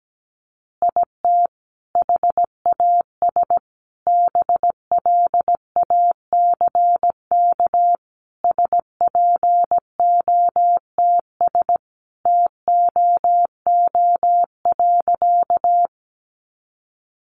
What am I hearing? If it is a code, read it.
Morse code, heard as a keyed tone, about 17 wpm, 700 Hz. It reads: IT HAS BLACK SPOTS TOO.
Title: Additional Easy Phrases 17wpm